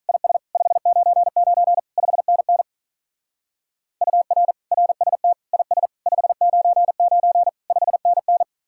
Morse code, heard as a keyed tone, about 43 wpm, 700 Hz. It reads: IS 599 5NN UR RST IS 599 5NN
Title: Speed-Racing Common QSO Phrases - 35wpm